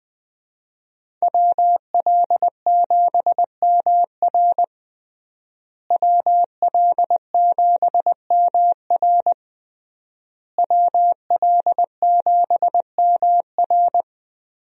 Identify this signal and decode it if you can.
Morse code, heard as a keyed tone, about 21 wpm, 700 Hz. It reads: WL7MR WL7MR WL7MR
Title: USA Call Signs - 5 Characters - Repeated 3x - 20wpm